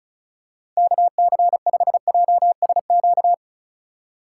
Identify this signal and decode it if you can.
Morse code, heard as a keyed tone, about 36 wpm, 700 Hz. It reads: KC5JSQ
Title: USA Call Signs - 6 Characters - 2x Character Spacing - 35wpm